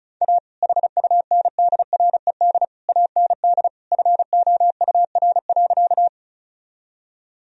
Morse code, heard as a keyed tone, about 36 wpm, 700 Hz. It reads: A HUNDRED AND FOUR.
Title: Sentences from Top 1000 Words - Review Part 2 - 35wpm